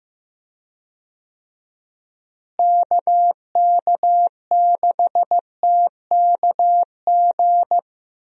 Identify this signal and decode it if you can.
Morse code, heard as a keyed tone, about 16 wpm, 700 Hz. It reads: KK6TKG
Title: USA Call Signs - 6 Characters - Repeated 3x - 15wpm